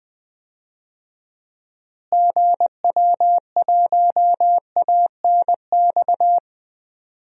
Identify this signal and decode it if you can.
Morse code, heard as a keyed tone, about 20 wpm, 700 Hz. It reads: GW1ANX
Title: DXCC-100 Call Signs - 6 Characters - Repeated 3x - 20wpm